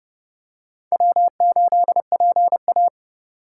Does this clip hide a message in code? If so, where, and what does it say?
Yes: Morse code from 0.9 to 2.9 s, heard as a keyed tone, about 30 wpm, 700 Hz. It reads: W8PA